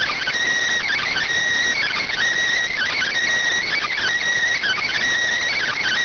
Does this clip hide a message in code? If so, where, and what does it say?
Yes: Emergency Alert System (SAME) tones from 0.8 to 4.9 s, heard as harsh buzzing bursts.